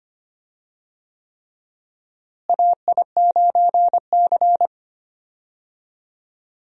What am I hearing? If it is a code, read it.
Morse code, heard as a keyed tone, about 25 wpm, 700 Hz. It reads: AI9C
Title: USA Call Signs - 4 Characters - Repeated 3x - 25wpm